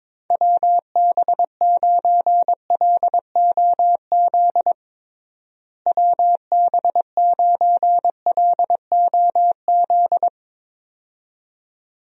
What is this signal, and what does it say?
Morse code, heard as a keyed tone, about 22 wpm, 700 Hz. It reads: WB9LOZ WB9LOZ
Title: USA Call Signs - 6 Characters - Repeated 2x - 22wpm